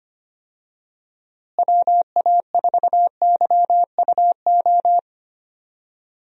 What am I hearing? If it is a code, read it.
Morse code, heard as a keyed tone, about 25 wpm, 700 Hz. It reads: WA4YUO
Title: USA Call Signs - 6 Characters - Repeated 2x - 25wpm